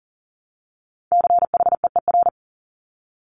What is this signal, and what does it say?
Morse code, heard as a keyed tone, about 42 wpm, 690 Hz. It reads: CHEER
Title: Multi-letter Phonemes Repeated 3x 40wpm